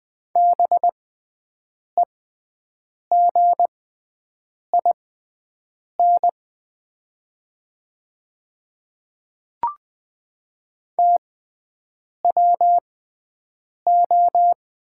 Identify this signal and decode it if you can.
Morse code, heard as a keyed tone, about 20 wpm, spaced as for 7 wpm, 700 Hz. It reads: BEGIN TWO
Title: 6x Character Spacing - Top 500 Words - 20wpm